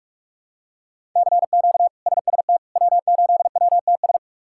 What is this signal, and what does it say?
Morse code, heard as a keyed tone, about 46 wpm, 700 Hz. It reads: CQ SST W8WTS
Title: K1USN SST QSO Practice - 45wpm